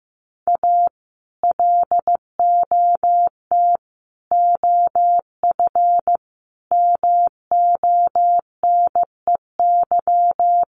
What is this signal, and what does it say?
Morse code, heard as a keyed tone, about 15 wpm, 700 Hz. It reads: A LOT OF MONEY
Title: Sets of 4 Words 15wpm